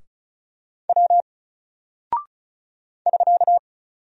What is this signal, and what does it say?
Morse code, heard as a keyed tone, about 36 wpm, 700 Hz. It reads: W <SK>
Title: Single Character (Letter:Number:Punctuation:Prosign) 35wpm